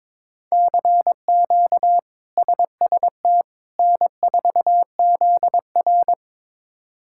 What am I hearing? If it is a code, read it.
Morse code, heard as a keyed tone, about 22 wpm, 700 Hz. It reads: CQ SST N4ZR
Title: K1USN SST QSO Practice - 22wpm